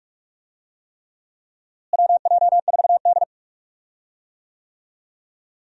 Morse code, heard as a keyed tone, about 46 wpm, 700 Hz. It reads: WJ4D
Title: USA Call Signs - 4 Characters - Repeated 3x - 45wpm